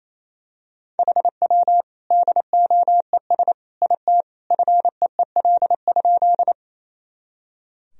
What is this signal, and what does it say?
Morse code, heard as a keyed tone, about 28 wpm, 700 Hz. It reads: HW DOES IT FEEL?
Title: Sentences from Top 400 Words and Abbreviations 28wpm